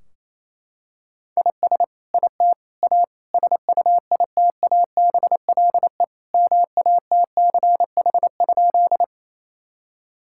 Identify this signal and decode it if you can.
Morse code, heard as a keyed tone, about 28 wpm, 700 Hz. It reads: IS IT A SUITABLE MATCH?
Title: Sentences from Top 1600 Words 28wpm